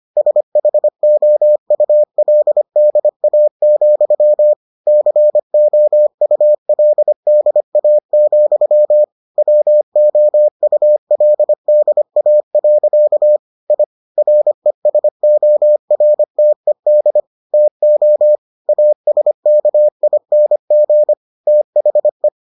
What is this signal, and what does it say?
Morse code, heard as a keyed tone, about 25 wpm, 600 Hz. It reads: SHOULDA, COULDA, WOULDA. I RESORTED TO ASKING THE